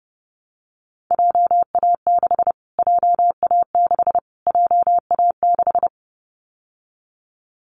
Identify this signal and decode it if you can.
Morse code, heard as a keyed tone, about 30 wpm, 700 Hz. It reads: JA6 JA6 JA6
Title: Callsign Prefixes for Top 25 DXCC - Repeated 3x - 30wpm